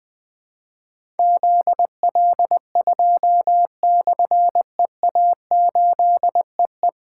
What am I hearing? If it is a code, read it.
Morse code, heard as a keyed tone, about 20 wpm, 700 Hz. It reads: ZL2/EA8EE
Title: Difficult Call Signs 20wpm